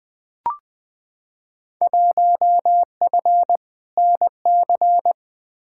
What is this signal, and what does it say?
Morse code, heard as a keyed tone, about 20 wpm, 700 Hz. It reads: 1F NC
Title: Field Day Exchanges - 20wpm